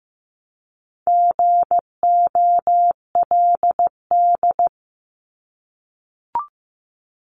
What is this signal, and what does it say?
Morse code, heard as a keyed tone, about 15 wpm, 700 Hz. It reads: GOLD
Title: Multi-letter Phonemes Repeated 3x 15wpm